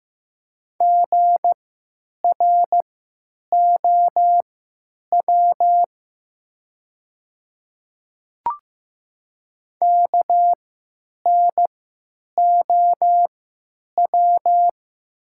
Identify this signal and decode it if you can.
Morse code, heard as a keyed tone, about 15 wpm, spaced as for 9 wpm, 700 Hz. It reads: GROW KNOW
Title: 3x Character Spacing - Top 300 Words - 15wpm